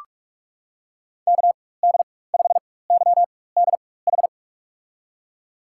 Morse code, heard as a keyed tone, about 43 wpm, spaced as for 21 wpm, 700 Hz. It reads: KD5YDH
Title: USA Call Signs - 6 Characters - 4x Character Spacing - 45wpm